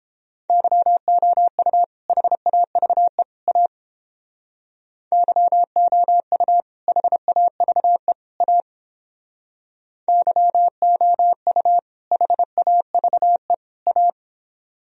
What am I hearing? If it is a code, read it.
Morse code, heard as a keyed tone, about 29 wpm, 700 Hz. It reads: YOU HAVE A YOU HAVE A YOU HAVE A